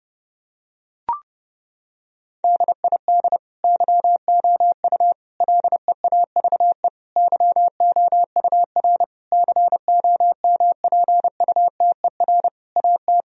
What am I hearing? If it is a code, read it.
Morse code, heard as a keyed tone, about 32 wpm, 700 Hz. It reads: DID YOU LEAVE YOUR COMPUTER AT
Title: Sentences from Top 1000 Words - Review Part 3 - 30wpm